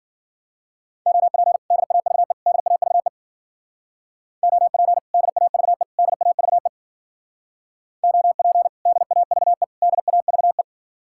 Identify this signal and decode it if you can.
Morse code, heard as a keyed tone, about 52 wpm, 700 Hz. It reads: OP DAVE DAVE OP DAVE DAVE OP DAVE DAVE